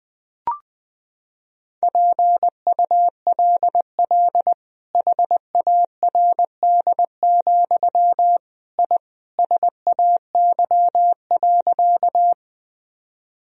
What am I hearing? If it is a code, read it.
Morse code, heard as a keyed tone, about 20 wpm, 700 Hz. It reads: PULL HARD, I SAY.
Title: Sentences from Top 1000 Words - Review Part 2 - 20wpm